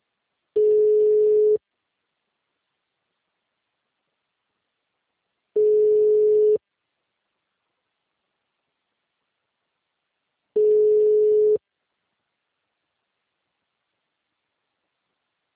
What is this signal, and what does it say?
Ringback tone (Europe, 425 Hz).